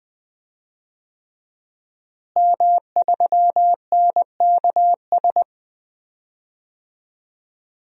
Morse code, heard as a keyed tone, about 20 wpm, 700 Hz. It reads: M3NKS
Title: DXCC-100 Call Signs - 5 Characters - Repeated 3x - 20wpm